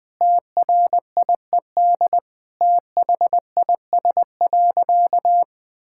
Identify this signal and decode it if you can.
Morse code, heard as a keyed tone, about 21 wpm, 700 Hz. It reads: TRIED THIS.
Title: Sentences with 4-Word Phrases - Spoken Beginning - 20wpm